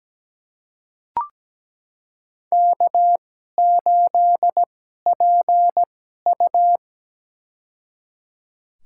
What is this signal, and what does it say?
Morse code, heard as a keyed tone, about 17 wpm, spaced as for 12 wpm, 700 Hz. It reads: K8PU
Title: USA Call Signs - 2x Character Spacing - 17wpm